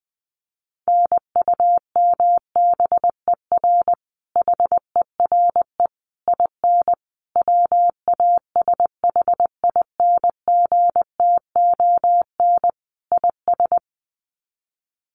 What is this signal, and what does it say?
Morse code, heard as a keyed tone, about 20 wpm, 700 Hz. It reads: NUMBER HERE IN WASHINGTON IS
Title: Sets of 5 Words Encore2 20wpm